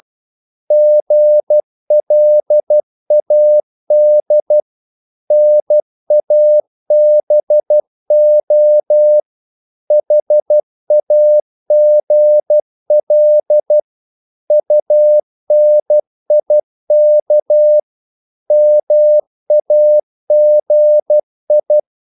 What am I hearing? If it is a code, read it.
Morse code, heard as a keyed tone, about 12 wpm, 600 Hz. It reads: GLAD NABO HAGL UNIK MAGI
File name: Korte ord DK 12wpm